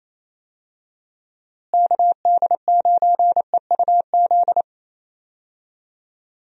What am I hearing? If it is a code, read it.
Morse code, heard as a keyed tone, about 27 wpm, 700 Hz. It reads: KD9EUZ
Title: USA Call Signs - 6 Characters - Repeated 3x - 28wpm